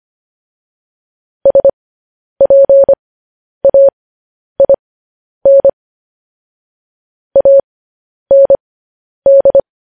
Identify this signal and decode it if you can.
Morse code, heard as a keyed tone, about 25 wpm, spaced as for 10 wpm, 550 Hz. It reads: SPAIN AND
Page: news headlines in morse code at 10 wpm